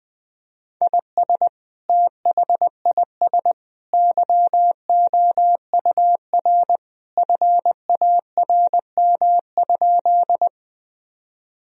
Morse code, heard as a keyed tone, about 21 wpm, 700 Hz. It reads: IS THIS YOUR FARM?
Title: Sentences from Top 1000 Words - Review Part 2 - 20wpm